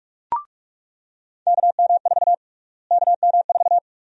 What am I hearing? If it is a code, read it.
Morse code, heard as a keyed tone, about 46 wpm, 700 Hz. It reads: KM4 KM4
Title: USA Call Sign Prefixes - Repeated 2x - 45wpm